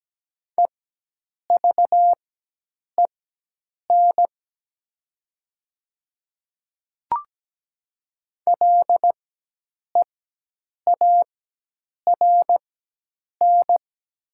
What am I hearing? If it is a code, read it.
Morse code, heard as a keyed tone, about 17 wpm, spaced as for 8 wpm, 700 Hz. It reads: EVEN LEARN
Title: 4x Character Spacing - Top 300 Words - 17wpm